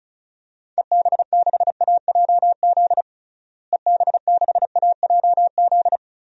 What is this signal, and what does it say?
Morse code, heard as a keyed tone, about 36 wpm, 700 Hz. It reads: EB6AJZ EB6AJZ
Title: DXCC-100 Call Signs - 6 Characters - Repeated 3x - 35wpm